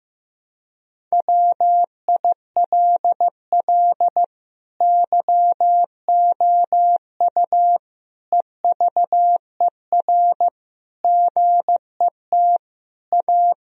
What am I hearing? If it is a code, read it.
Morse code, heard as a keyed tone, about 16 wpm, 700 Hz. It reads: WILL YOU EVER GET A
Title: Sentences from Top 1000 Words - Review Part 3 - 15wpm